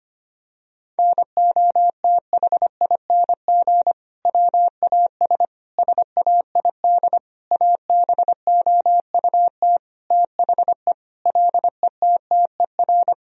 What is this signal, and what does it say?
Morse code, heard as a keyed tone, about 25 wpm, 700 Hz. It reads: NOTHING WAS SAID ABOUT THE LETTER
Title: Sentences from Top 1000 Words - Review Part 2 - 25wpm